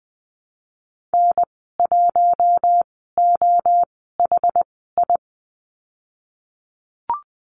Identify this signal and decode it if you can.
Morse code, heard as a keyed tone, about 20 wpm, spaced as for 15 wpm, 700 Hz. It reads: N1OHI
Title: 088 Lesson 15g - Number 5 - Call Signs 20wpm